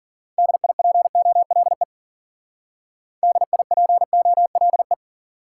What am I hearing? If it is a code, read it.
Morse code, heard as a keyed tone, about 50 wpm, 700 Hz. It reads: DIPOLE DIPOLE